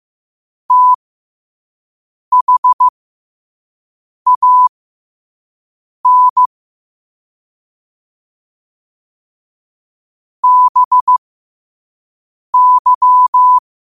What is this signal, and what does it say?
Morse code, heard as a keyed tone, about 15 wpm, spaced as for 5 wpm, 1000 Hz. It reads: THAN BY
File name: qotdfiles / 5wpm